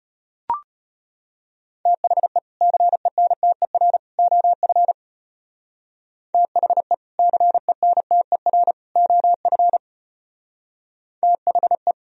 Morse code, heard as a keyed tone, about 32 wpm, 700 Hz. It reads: THE CENTER OF THE CENTER OF THE